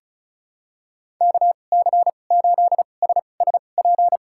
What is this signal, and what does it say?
Morse code, heard as a keyed tone, about 36 wpm, spaced as for 25 wpm, 700 Hz. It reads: KC8SSP
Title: USA Call Signs - 6 Characters - 2x Character Spacing - 35wpm